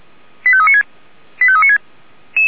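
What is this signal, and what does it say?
Five-tone selective call (ZVEI), heard as quick runs of five tones, tones of about 70 ms. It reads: 85396 85396
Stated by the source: Five-tone ZVEI alert received by SDR for the Feuerwehr Orlamünde fire brigade.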